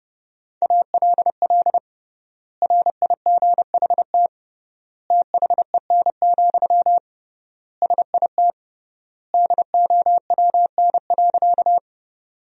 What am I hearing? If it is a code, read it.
Morse code, heard as a keyed tone, about 30 wpm, 700 Hz. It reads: ALL RIGHT THEN, SIT DOWN.
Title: 3x Word-Spacing - Sentences from Top 500 Words - 30wpm